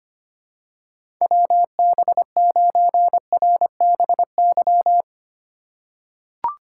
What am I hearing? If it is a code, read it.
Morse code, heard as a keyed tone, about 25 wpm, 700 Hz. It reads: WB9RBY
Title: USA Call Signs - 2x Character Spacing - 25wpm